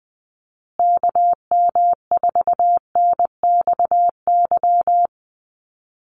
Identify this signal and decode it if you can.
Morse code, heard as a keyed tone, about 20 wpm, 700 Hz. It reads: KM4NXY
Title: USA Call Signs Repeated 2x 20wpm